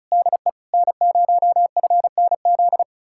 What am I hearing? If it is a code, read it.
Morse code, heard as a keyed tone, about 46 wpm, spaced as for 41 wpm, 700 Hz. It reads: DE N0FNZ
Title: Speed-Racing Common QSO Phrases - 35wpm